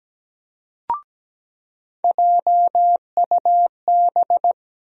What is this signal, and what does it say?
Morse code, heard as a keyed tone, about 17 wpm, 700 Hz. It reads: JUB
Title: USA Call Sign Postfixes - 3 Letters 17wpm